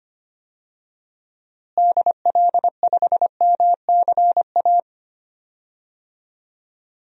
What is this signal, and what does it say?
Morse code, heard as a keyed tone, about 25 wpm, 700 Hz. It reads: DL5MCA
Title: DXCC-100 Call Signs - 6 Characters - Repeated 2x - 25wpm